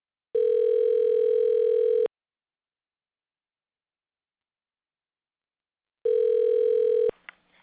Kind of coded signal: telephone call progress tone